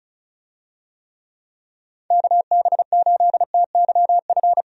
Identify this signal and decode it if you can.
Morse code, heard as a keyed tone, about 36 wpm, 700 Hz. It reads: KB8TYF
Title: USA Call Signs - 6 Characters - Repeated 2x - 35wpm